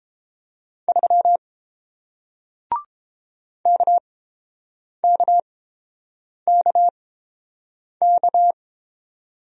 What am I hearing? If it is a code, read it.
Morse code, heard as a keyed tone, about 30 wpm, spaced as for 7 wpm, 700 Hz. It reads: 3 KKKK